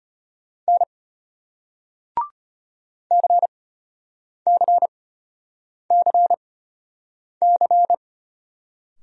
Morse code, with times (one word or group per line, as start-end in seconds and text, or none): 0.7-0.8 N
3.1-7.9 CCCC